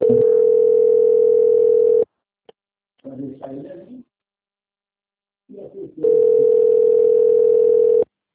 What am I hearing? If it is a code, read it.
Ringback tone (North America), 440 + 480 Hz.
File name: out- - - - - .19570